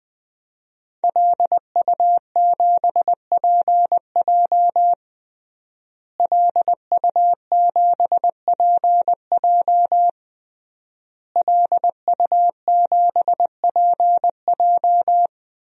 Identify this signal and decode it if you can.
Morse code, heard as a keyed tone, about 20 wpm, 700 Hz. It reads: LU7PJ LU7PJ LU7PJ
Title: DXCC-100 Call Signs - 5 Characters - Repeated 3x - 20wpm